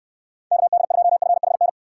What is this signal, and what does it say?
Morse code, heard as a keyed tone, about 69 wpm, 700 Hz. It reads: KD2FHN